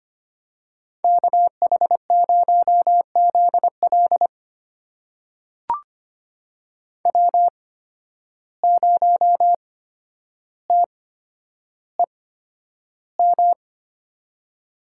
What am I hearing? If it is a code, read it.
Morse code, heard as a keyed tone, about 25 wpm, spaced as for 11 wpm, 700 Hz. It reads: KH0ZL W0TEM
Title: USA Call Signs - 5 Characters - 8x Character Spacing - 25wpm